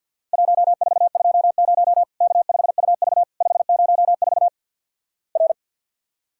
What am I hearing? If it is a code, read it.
Morse code, heard as a keyed tone, about 50 wpm, 700 Hz. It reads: 1420 K5UV 504 R
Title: Worked All Europe DX Contest - QTC Exchange - 50wpm